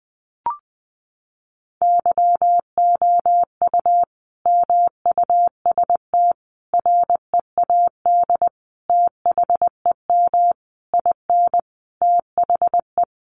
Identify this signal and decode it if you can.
Morse code, heard as a keyed tone, about 20 wpm, 700 Hz. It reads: YOU MUST READ THEM IN THE
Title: Sentences from Top 900 Words 20wpm